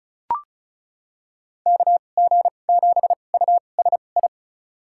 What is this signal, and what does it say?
Morse code, heard as a keyed tone, about 36 wpm, spaced as for 25 wpm, 700 Hz. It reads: KG7USI
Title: USA Call Signs - 6 Characters - 2x Character Spacing - 35wpm